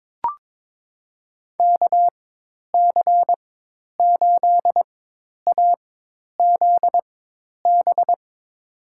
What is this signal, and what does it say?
Morse code, heard as a keyed tone, about 22 wpm, spaced as for 10 wpm, 700 Hz. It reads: KC8AZB
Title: USA Call Signs - 6 Characters - 4x Character Spacing - 22wpm